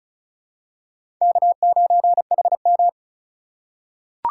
Morse code, heard as a keyed tone, about 36 wpm, 700 Hz. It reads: K9HM
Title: USA Call Signs - 4 Characters - 3x Character Spacing - 35wpm